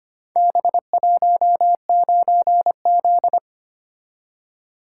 Morse code, heard as a keyed tone, about 25 wpm, 700 Hz. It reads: B19Z
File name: practice / 25wpm / Four Letter-Number Combinations - 25wpm